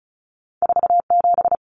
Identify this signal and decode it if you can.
Morse code, heard as a keyed tone, about 36 wpm, 700 Hz. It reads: 47
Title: Numbers - 2-Digit 35wpm